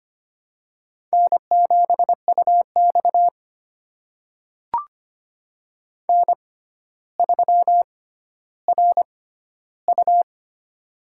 Morse code, heard as a keyed tone, about 25 wpm, spaced as for 13 wpm, 700 Hz. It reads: N7UX N3RU
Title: USA Call Signs - 4 Characters - 6x Character Spacing - 25wpm